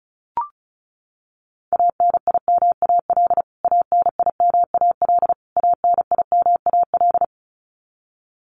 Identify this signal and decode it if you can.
Morse code, heard as a keyed tone, about 34 wpm, 700 Hz. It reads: ANIMAL ANIMAL ANIMAL
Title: Top 200 Words Repeated 35wpm